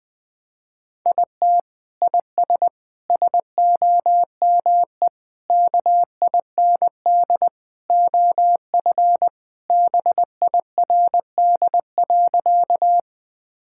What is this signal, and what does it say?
Morse code, heard as a keyed tone, about 21 wpm, 700 Hz. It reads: IT IS SOME KIND OF BIRD.
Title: Sentences with 3-Word Phrases - 20wpm